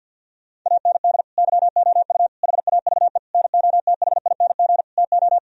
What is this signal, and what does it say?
Morse code, heard as a keyed tone, about 52 wpm, 700 Hz. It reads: AND YOU HAVE NOTHING TO
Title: Sentences from Top 1000 Words - Review Part 1 - 50wpm